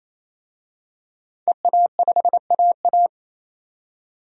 Morse code, heard as a keyed tone, about 28 wpm, 700 Hz. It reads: EA5AA
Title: DXCC-100 Call Signs - 5 Characters - 28wpm